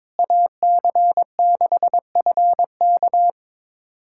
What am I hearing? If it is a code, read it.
Morse code, heard as a keyed tone, about 23 wpm, 700 Hz. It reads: AC6FK
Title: USA Callsign DE Callsign - 5 Characters - 22wpm